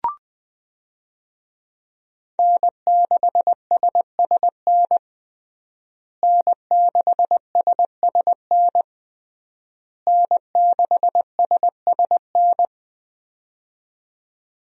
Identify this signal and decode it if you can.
Morse code, heard as a keyed tone, about 21 wpm, 700 Hz. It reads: N6SSN N6SSN N6SSN
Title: USA Call Signs - 5 Characters - Repeated 3x - 20wpm